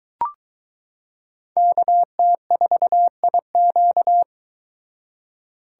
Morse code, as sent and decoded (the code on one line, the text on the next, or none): -.- - ....- .. --.-
KT4IQ